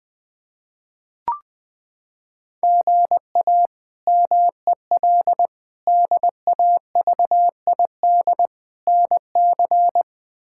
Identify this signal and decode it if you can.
Morse code, heard as a keyed tone, about 20 wpm, 700 Hz. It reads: GA MEL DAVID NC
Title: K1USN SST QSO Practice - 20wpm